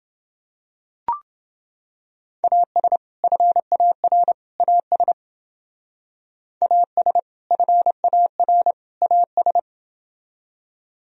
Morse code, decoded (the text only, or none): AS FAR AS AS FAR AS